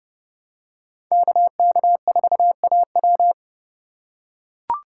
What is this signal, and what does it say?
Morse code, heard as a keyed tone, about 29 wpm, 700 Hz. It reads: KK4AW